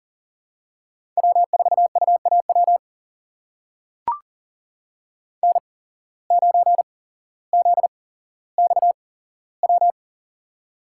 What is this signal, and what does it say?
Morse code, heard as a keyed tone, about 38 wpm, spaced as for 17 wpm, 700 Hz. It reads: W4UAW N9ZXW
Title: USA Call Signs - 5 Characters - 8x Character Spacing - 40wpm